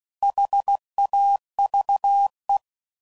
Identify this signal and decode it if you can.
Morse code, heard as a keyed tone, about 16 wpm, 800 Hz. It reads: HAVE
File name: CW Code Groups @15WPM / Weather Sets / Weather 5 5